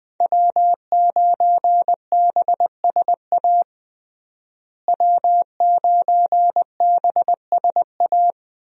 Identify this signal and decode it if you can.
Morse code, heard as a keyed tone, about 20 wpm, 700 Hz. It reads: W9BSA W9BSA
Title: USA Call Signs - 5 Characters - Repeated 2x - 20wpm